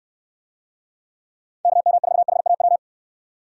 Morse code, heard as a keyed tone, about 68 wpm, 700 Hz. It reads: KM4HAW